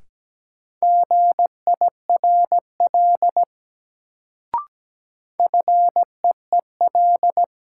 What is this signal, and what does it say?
Morse code, heard as a keyed tone, about 17 wpm, 700 Hz. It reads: GIRL FEEL
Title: Sentences from Top 400 Words - New Word Drill - 17wpm